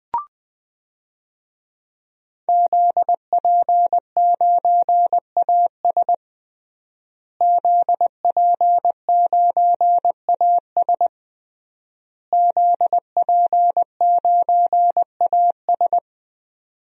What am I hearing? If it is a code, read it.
Morse code, heard as a keyed tone, about 20 wpm, 700 Hz. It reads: ZP9AS ZP9AS ZP9AS
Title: DXCC-100 Call Signs - 5 Characters - Repeated 3x - 20wpm